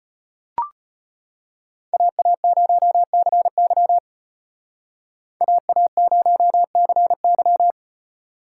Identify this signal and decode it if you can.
Morse code, heard as a keyed tone, about 35 wpm, 700 Hz. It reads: AA0CY AA0CY